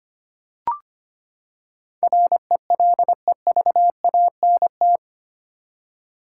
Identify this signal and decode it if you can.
Morse code, heard as a keyed tone, about 25 wpm, 700 Hz. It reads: RELEVANT
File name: Eight Letter Words - 25wpm